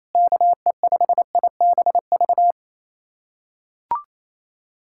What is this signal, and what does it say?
Morse code, heard as a keyed tone, about 28 wpm, 700 Hz. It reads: KE5IBV
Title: USA Callsign DE Callsign 28wpm